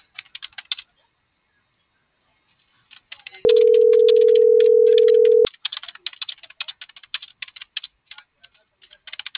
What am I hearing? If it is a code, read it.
Ringback tone (North America), 440 + 480 Hz.